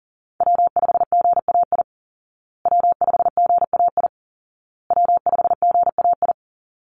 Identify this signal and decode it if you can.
Morse code, heard as a keyed tone, about 37 wpm, 700 Hz. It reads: W5GAI W5GAI W5GAI
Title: Call Signs Repeated 2x - 40wpm